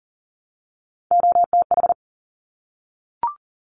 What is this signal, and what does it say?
Morse code, heard as a keyed tone, about 39 wpm, 700 Hz. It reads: OTH
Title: Top Three Letter Combinations in Words 40wpm